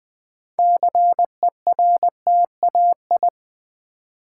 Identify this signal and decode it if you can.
Morse code, heard as a keyed tone, about 21 wpm, 700 Hz. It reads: CERTAI
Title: Build A Word - 3 Letters At A Time - 9-Letter Words - 20wpm